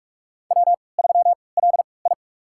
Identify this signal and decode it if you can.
Morse code, heard as a keyed tone, about 46 wpm, spaced as for 26 wpm, 690 Hz. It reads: W3LI
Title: USA Call Signs - 4 Characters - 3x Character Spacing - 45wpm